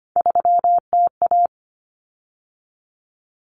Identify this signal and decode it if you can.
Morse code, heard as a keyed tone, about 25 wpm, 700 Hz. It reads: 3TA
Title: Common Cut Numbers - 3-Digit 25wpm